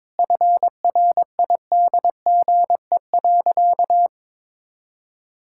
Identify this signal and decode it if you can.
Morse code, heard as a keyed tone, about 22 wpm, 690 Hz. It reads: FRIDGE.
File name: practice / 22wpm / Easy Sentences with Hints - 22wpm